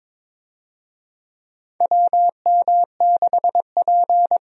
Morse code, heard as a keyed tone, about 22 wpm, 700 Hz. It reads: WM6P
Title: USA Call Signs - 4 Characters - Repeated 3x - 22wpm